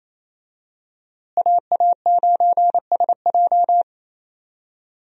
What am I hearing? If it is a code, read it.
Morse code, heard as a keyed tone, about 28 wpm, 700 Hz. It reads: AA9SJ
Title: USA Call Signs - 5 Characters - Repeated 3x - 28wpm